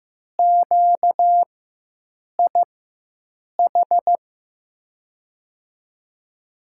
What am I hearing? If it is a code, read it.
Morse code, heard as a keyed tone, about 15 wpm, spaced as for 7 wpm, 700 Hz. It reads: QIH
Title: USA Call Signs - 6 Characters - 4x Character Spacing - 15wpm